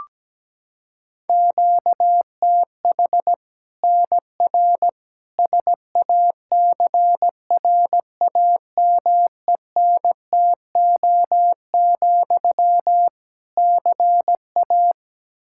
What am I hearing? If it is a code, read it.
Morse code, heard as a keyed tone, about 17 wpm, 700 Hz. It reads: QTH NR SACRAMENTO, CA
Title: Practice QSO - Exchange 1 - Beginning - 17wpm